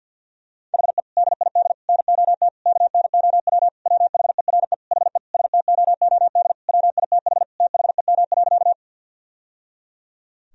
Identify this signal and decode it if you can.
Morse code, heard as a keyed tone, about 51 wpm, 700 Hz. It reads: HE DID NOT KNOW WHERE HE STOOD WITH THEM.
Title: Sentences from Top 1000 Words - Review Part 2 - 50wpm